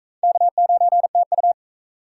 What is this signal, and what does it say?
Morse code, heard as a keyed tone, about 38 wpm, 700 Hz. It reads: K9TU